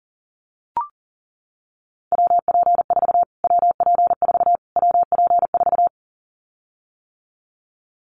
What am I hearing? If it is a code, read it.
Morse code, heard as a keyed tone, about 37 wpm, 700 Hz. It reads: WP4 WP4 WP4
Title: Callsign Prefixes for Top 25 DXCC - Repeated 3x - 40wpm